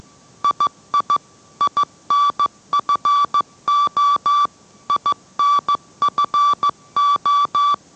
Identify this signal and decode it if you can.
Morse code, heard as a keyed tone, about 15 wpm, 1160 Hz. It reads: II INFO INFO